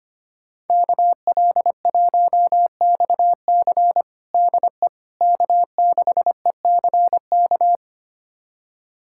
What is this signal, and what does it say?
Morse code, heard as a keyed tone, about 25 wpm, 700 Hz. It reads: KL1XC DE K6ECK
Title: USA Callsign DE Callsign - 5 Characters - 25wpm